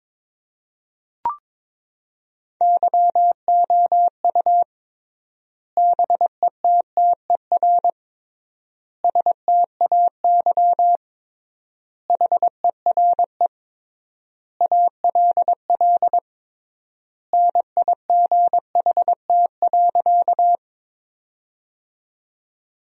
Morse code, heard as a keyed tone, about 22 wpm, 700 Hz. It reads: YOU BETTER STAY HERE ALL NIGHT.
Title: 3x Word-Spacing - Sentences from Top 500 Words - 22wpm